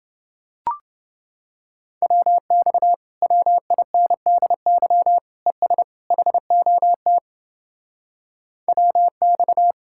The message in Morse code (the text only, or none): WX WINDY ES HOT WX